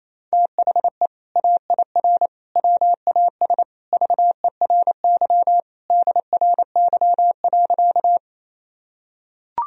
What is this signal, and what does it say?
Morse code, heard as a keyed tone, about 28 wpm, 700 Hz. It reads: THE AIR WAS VERY DRY.
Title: Sentences from Top 1000 Words - Review Part 3 - 28wpm